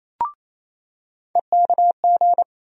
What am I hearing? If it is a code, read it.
Morse code, heard as a keyed tone, about 27 wpm, 700 Hz. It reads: EKG
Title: USA Call Sign Postfixes - 3 Letters 28wpm